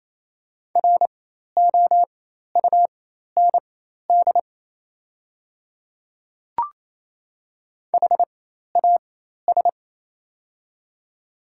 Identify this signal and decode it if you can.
Morse code, heard as a keyed tone, about 28 wpm, spaced as for 13 wpm, 700 Hz. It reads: ROUND HAS
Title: 4x Character Spacing - Top 300 Words - 28wpm